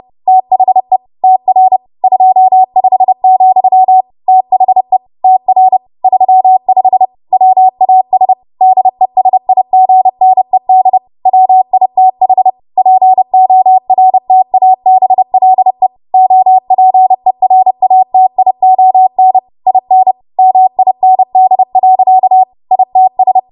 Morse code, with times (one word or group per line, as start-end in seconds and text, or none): 0.3-1.0 THE
1.2-1.8 TR
2.0-4.0 25,
4.3-5.0 THE
5.2-5.8 TR
6.0-7.1 35
7.3-8.3 WAS
8.6-11.0 DESIGNED
11.3-12.5 WITH
12.8-15.9 PORTABLE
16.1-19.4 OPERATION
19.7-20.1 IN
20.4-22.4 MIND.
22.7-23.4 ITS